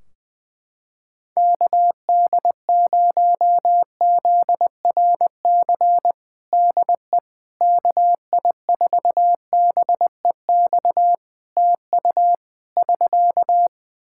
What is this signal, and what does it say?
Morse code, heard as a keyed tone, about 20 wpm, 700 Hz. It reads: KD0ZRC DE KI4BEX TU <SK>
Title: Practice QSO - Exchange 4 - Ending - 20wpm